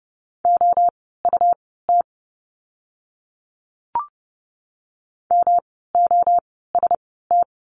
Morse code, heard as a keyed tone, about 30 wpm, spaced as for 17 wpm, 700 Hz. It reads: OUT MOST
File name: course / 30:10wpm / 237 Lesson 38f - Letter q - Words 30wpm